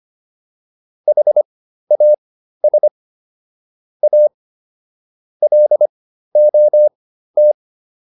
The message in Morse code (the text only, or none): HAS A LOT